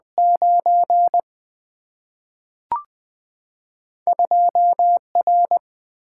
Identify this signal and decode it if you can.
Morse code, heard as a keyed tone, about 20 wpm, 700 Hz. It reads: 9 2 R
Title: Single-Letter Single-Number 20wpm